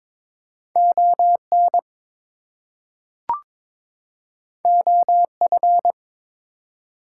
Morse code, heard as a keyed tone, about 23 wpm, 700 Hz. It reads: ON OF
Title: Two Letter Words - 22wpm